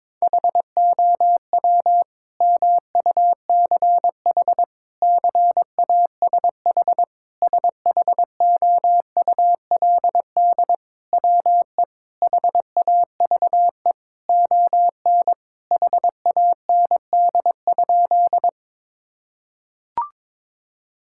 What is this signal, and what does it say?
Morse code, heard as a keyed tone, about 22 wpm, 700 Hz. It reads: HOW MUCH CASH SHOULD WE HAVE ON HAND?
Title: Sentences from Top 1000 Words - Review Part 5 - 22wpm